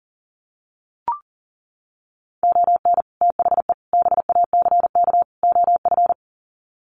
Morse code, heard as a keyed tone, about 40 wpm, 700 Hz. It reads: ON THE BACK OF
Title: Sets of 4 Words 40wpm